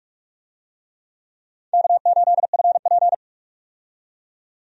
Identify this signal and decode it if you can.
Morse code, heard as a keyed tone, about 42 wpm, 700 Hz. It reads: K8FP
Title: USA Call Signs - 4 Characters - 45wpm